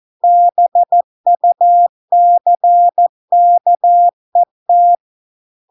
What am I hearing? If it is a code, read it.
Morse code, heard as a keyed tone, about 14 wpm, 700 Hz. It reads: BUCKET